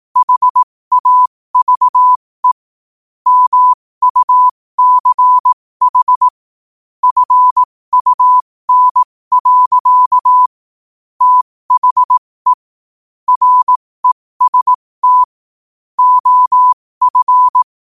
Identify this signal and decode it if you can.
Morse code, heard as a keyed tone, about 18 wpm, spaced as for 15 wpm, 1000 Hz. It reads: HAVE MUCH FUN. THE REST OF